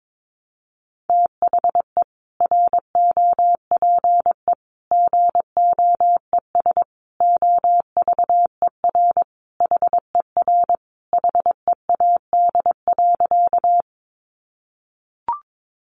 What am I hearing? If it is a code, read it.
Morse code, heard as a keyed tone, about 22 wpm, 700 Hz. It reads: THE ROPE GOES OVER HER HEAD.
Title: Easy Phrases 22wpm